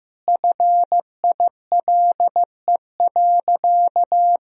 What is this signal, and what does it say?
Morse code, heard as a keyed tone, about 16 wpm, 700 Hz. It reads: FILE.
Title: Sentences from Top 1000 Words - Review Part 4 - 15wpm